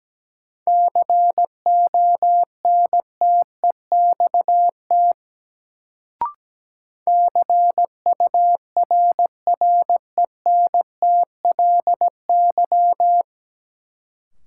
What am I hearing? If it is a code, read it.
Morse code, heard as a keyed tone, about 17 wpm, 700 Hz. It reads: CONTEXT CURRENTLY
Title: Sentences from Top 1000 Words - Review Part 7 - New Word Drill - 17wpm